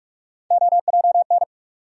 Morse code, heard as a keyed tone, about 46 wpm, 700 Hz. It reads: OJN
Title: USA Call Sign Postfixes - 3 Letters 45wpm